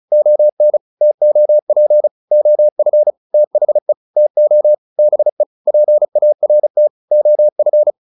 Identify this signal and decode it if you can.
Morse code, heard as a keyed tone, about 35 wpm, 600 Hz. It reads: ON TOP OF THE TO BE PART OF